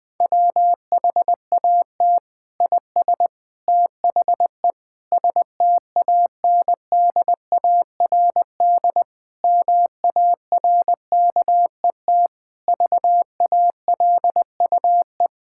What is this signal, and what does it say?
Morse code, heard as a keyed tone, about 20 wpm, 700 Hz. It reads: WHAT IS THE STANDARD MARKET VALUE
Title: Sentences from Top 1000 Words - Review Part 6 - 20wpm